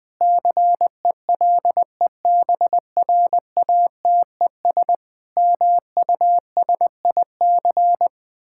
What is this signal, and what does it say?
Morse code, heard as a keyed tone, about 20 wpm, 700 Hz. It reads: CELEBRATES MUSIC
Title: Sentences with 7-Word Phrases - Spoken Beginning - 20wpm